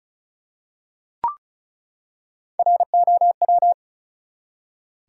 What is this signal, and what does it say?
Morse code, heard as a keyed tone, about 36 wpm, 700 Hz. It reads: ROW
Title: Three Letter Words - 35wpm